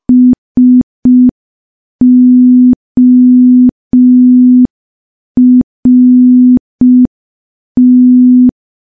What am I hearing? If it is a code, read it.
Morse code, heard as a keyed tone, about 5 wpm, 250 Hz. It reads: SORT
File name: morse / morse-training-05WPM